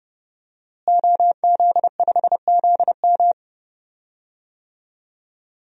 Morse code, heard as a keyed tone, about 32 wpm, 700 Hz. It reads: OZ5ZM
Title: DXCC-100 Call Signs - 5 Characters - Repeated 3x - 30wpm